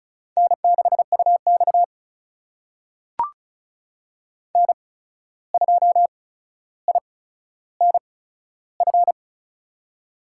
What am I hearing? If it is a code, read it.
Morse code, heard as a keyed tone, about 36 wpm, spaced as for 14 wpm, 700 Hz. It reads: N6UX N2INF
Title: USA Call Signs - 8x Character Spacing - 35wpm